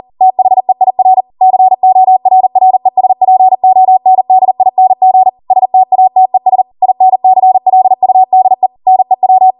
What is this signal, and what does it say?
Morse code, heard as a keyed tone, about 40 wpm, 750 Hz. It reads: THEIR CORRESPONDING STATES INCLUDE NEW